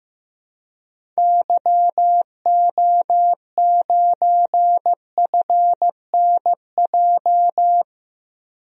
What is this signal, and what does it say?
Morse code, heard as a keyed tone, about 16 wpm, 700 Hz. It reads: YO9FNJ
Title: DXCC-100 Call Signs - 6 Characters - 15wpm